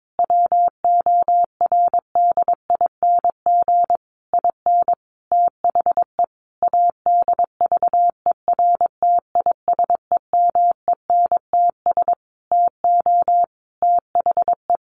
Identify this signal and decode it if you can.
Morse code, heard as a keyed tone, about 22 wpm, 700 Hz. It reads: WORDING IN THE ADVERTISEMENTS TO THE
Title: A Sound Of Thunder 22wpm